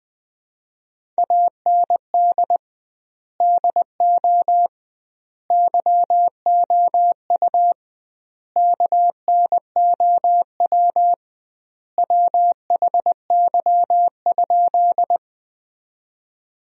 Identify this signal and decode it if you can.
Morse code, heard as a keyed tone, about 20 wpm, 700 Hz. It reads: AND DO YOU KNOW WHY?
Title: 2x Word-Spacing - Sentences from Top 200 Words - 20wpm